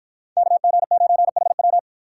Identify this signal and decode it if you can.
Morse code, heard as a keyed tone, about 48 wpm, 700 Hz. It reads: KG9HW